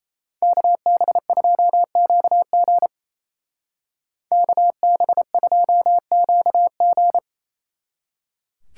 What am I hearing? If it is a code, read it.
Morse code, heard as a keyed tone, about 37 wpm, 700 Hz. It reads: KB2QG KB2QG